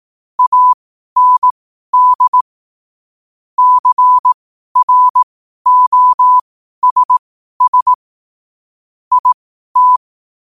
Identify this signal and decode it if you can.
Morse code, heard as a keyed tone, about 18 wpm, spaced as for 12 wpm, 1000 Hz. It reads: AND CROSS IT
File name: qotdfiles / 12wpm